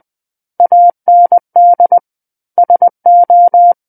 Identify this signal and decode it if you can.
Morse code, heard as a keyed tone, about 20 wpm, 700 Hz. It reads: AND SO
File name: 20 WPM